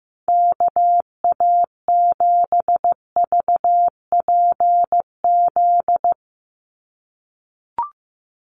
Morse code, heard as a keyed tone, about 15 wpm, 700 Hz. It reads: KA7VPZ
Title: USA Call Signs 15wpm